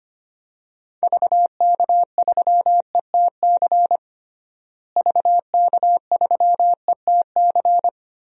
Morse code, heard as a keyed tone, about 25 wpm, 700 Hz. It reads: VK3ETC VK3ETC
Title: DXCC-100 Call Signs - 6 Characters - Repeated 3x - 25wpm